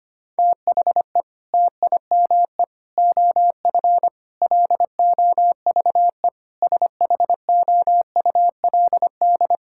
Morse code, heard as a keyed tone, about 25 wpm, 700 Hz. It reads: THE TIME OF LOVE SHOULD
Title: Sentences from Top 1600 Words 25wpm